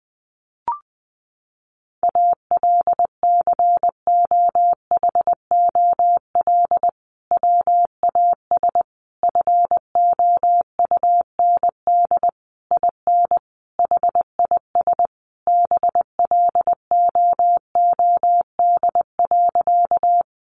Morse code, heard as a keyed tone, about 20 wpm, 700 Hz. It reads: ALCOHOL WAS FOUND IN HIS BLOOD.
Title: Sentences from Top 1400 Words 20wpm